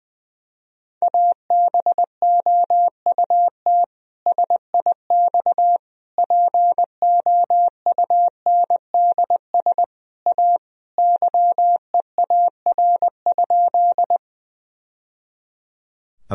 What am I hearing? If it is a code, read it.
Morse code, heard as a keyed tone, about 20 wpm, 700 Hz. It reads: ABOUT SIX POUNDS A YEAR?
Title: Sentences from Top 1000 Words - Review Part 2 - 20wpm